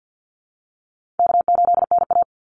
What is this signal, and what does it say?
Morse code, heard as a keyed tone, about 53 wpm, 700 Hz. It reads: K8NA
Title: Easy Call Signs - 50wpm